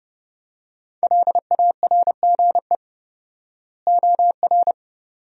Morse code, heard as a keyed tone, about 30 wpm, 700 Hz. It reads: LARGE OR
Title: 4x Word-Spacing - Sentences from Top 200 Words - 30wpm